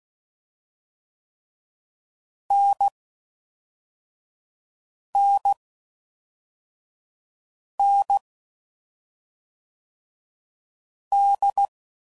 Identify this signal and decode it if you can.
Morse code, heard as a keyed tone, about 16 wpm, 800 Hz. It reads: N N N D
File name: NDB